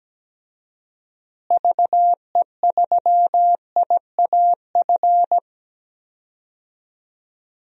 Morse code, heard as a keyed tone, about 17 wpm, 700 Hz. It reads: VE3IAF
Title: DXCC-100 Call Signs - 6 Characters - Repeated 2x - 17wpm